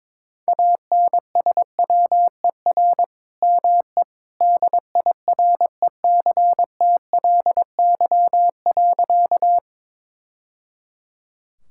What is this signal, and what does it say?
Morse code, heard as a keyed tone, about 22 wpm, 700 Hz. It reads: ANSWER ME DIRECTLY.
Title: Sentences from Top 1000 Words - Review Part 5 - 22wpm